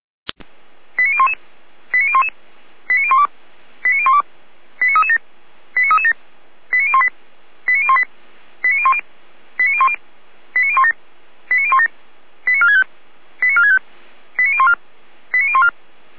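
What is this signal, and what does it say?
Five-tone selective call (ZVEI), heard as quick runs of five tones, tones of about 70 ms. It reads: 89011 89011 89012 89012 89337 89337 89018 89018 89010 89010 89017 89017 89465 89465 89014 89014
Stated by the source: ZVEI five-tone selective call 89330, received by SDR.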